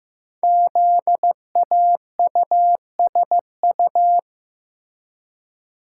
Morse code, heard as a keyed tone, about 15 wpm, 700 Hz. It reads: ZAUSU
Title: Code Groups - 5 Letters - 15wpm